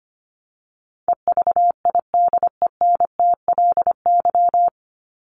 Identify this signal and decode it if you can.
Morse code, heard as a keyed tone, about 25 wpm, 700 Hz. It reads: EVIDENTLY
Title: LY Suffix - Top 100 Words with Base Words 25wpm